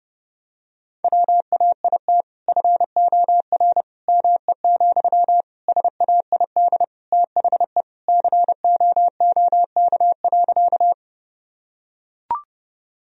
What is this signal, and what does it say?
Morse code, heard as a keyed tone, about 31 wpm, 700 Hz. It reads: WAIT FOR ME, SAID THE COOK.
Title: Sentences from Top 1000 Words - Review Part 4 - 30wpm